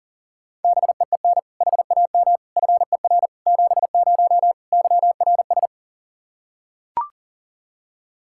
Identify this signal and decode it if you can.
Morse code, heard as a keyed tone, about 37 wpm, 700 Hz. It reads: BEEN HAM FER 70 YRS
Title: Practice QSO - Exchange 3 - More Info - 40wpm